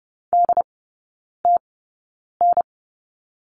Morse code, heard as a keyed tone, about 30 wpm, spaced as for 9 wpm, 690 Hz. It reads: DTN
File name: Lesson 16e - Letter c - Three Characters 30wpm